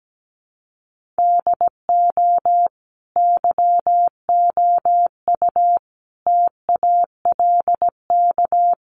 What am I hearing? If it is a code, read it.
Morse code, heard as a keyed tone, about 17 wpm, 700 Hz. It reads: DO YOU TALK
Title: Sentences from Top 500 Words 17wpm